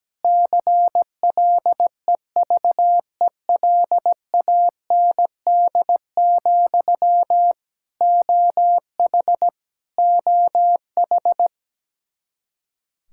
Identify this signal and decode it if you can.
Morse code, heard as a keyed tone, about 17 wpm, 700 Hz. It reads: CLEVELAND, OH OH
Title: K1USN SST QSO Practice 17wpm